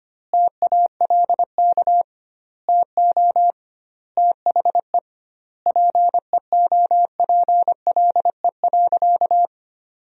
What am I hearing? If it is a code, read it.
Morse code, heard as a keyed tone, about 25 wpm, 700 Hz. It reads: TALK TO THE PEOPLE.
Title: 2x Word-Spacing - Sentences from Top 400 Words - 25wpm